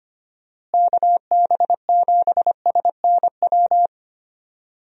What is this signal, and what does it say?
Morse code, heard as a keyed tone, about 25 wpm, 700 Hz. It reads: KB7SNW
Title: USA Call Signs - 6 Characters - Repeated 3x - 25wpm